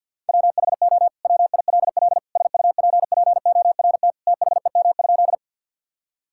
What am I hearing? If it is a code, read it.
Morse code, heard as a keyed tone, about 52 wpm, 700 Hz. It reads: WHO WILL SUPPORT THEM?
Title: Sentences from Top 1000 Words - Review Part 5 - 50wpm